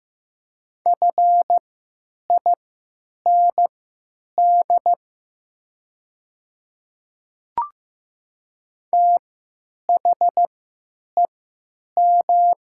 Morse code, heard as a keyed tone, about 15 wpm, spaced as for 9 wpm, 700 Hz. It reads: FIND THEM
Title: 3x Character Spacing - Top 100 Words - 15wpm